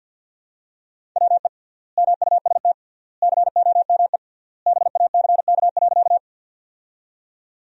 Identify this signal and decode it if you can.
Morse code, heard as a keyed tone, about 53 wpm, 700 Hz. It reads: WE MUST COME BACK.
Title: 3x Word-Spacing - Sentences from Top 200 Words - 50wpm